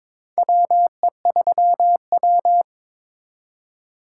Morse code, heard as a keyed tone, about 22 wpm, 700 Hz. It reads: WE3W
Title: USA Call Signs - 4 Characters - 8x Character Spacing - 22wpm